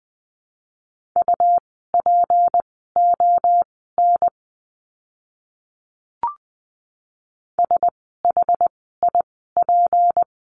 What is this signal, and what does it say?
Morse code, heard as a keyed tone, about 20 wpm, spaced as for 14 wpm, 700 Hz. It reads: UPON SHIP
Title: 153 Lesson 25f - Letter p - Words 20wpm